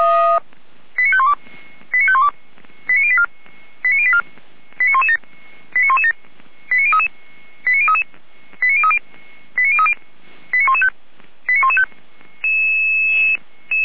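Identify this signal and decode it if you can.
Five-tone selective call (ZVEI), heard as quick runs of five tones, tones of about 80 ms. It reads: 89512 89512 89084 89084 89117 89117 89033 89033 89030 89030 89115 89115
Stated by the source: ZVEI five-tone selective call 89086 received by SDR.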